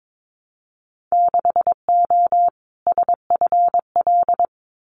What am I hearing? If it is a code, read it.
Morse code, heard as a keyed tone, about 22 wpm, 700 Hz. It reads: 6O SFL
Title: Winter Field Day Exchanges Encore 22wpm